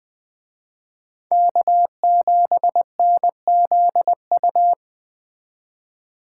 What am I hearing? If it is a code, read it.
Morse code, heard as a keyed tone, about 20 wpm, 700 Hz. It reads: K7NZU